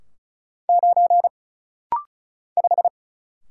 Morse code, heard as a keyed tone, about 34 wpm, 700 Hz. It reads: 9 5
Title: Single Letter-Number - Rapid-Fire 35wpm